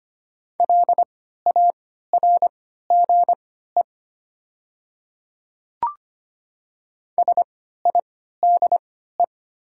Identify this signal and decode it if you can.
Morse code, heard as a keyed tone, about 25 wpm, spaced as for 14 wpm, 700 Hz. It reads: LARGE SIDE
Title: 3x Character Spacing - Top 200 Words - 25wpm